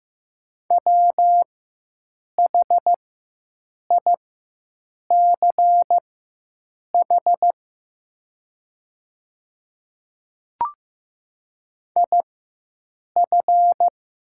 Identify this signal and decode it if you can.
Morse code, heard as a keyed tone, about 15 wpm, spaced as for 7 wpm, 700 Hz. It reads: WHICH IF
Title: 4x Character Spacing - Top 100 Words - 15wpm